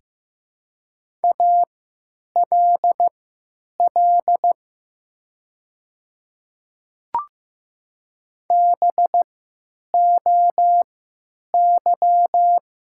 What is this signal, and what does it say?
Morse code, heard as a keyed tone, about 15 wpm, spaced as for 9 wpm, 700 Hz. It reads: ALL BOY
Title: 3x Character Spacing - Top 300 Words - 15wpm